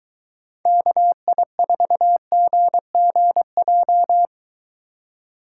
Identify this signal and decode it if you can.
Morse code, heard as a keyed tone, about 22 wpm, 700 Hz. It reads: KI4GGJ